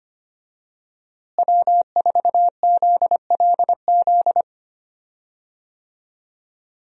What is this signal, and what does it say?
Morse code, heard as a keyed tone, about 25 wpm, 700 Hz. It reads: W4ZLZ
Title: USA Call Signs - 5 Characters - Repeated 3x - 25wpm